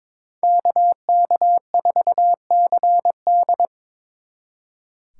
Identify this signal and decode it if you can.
Morse code, heard as a keyed tone, about 26 wpm, 700 Hz. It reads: KK4CD
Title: Speed Racing - USA Call Signs 22wpm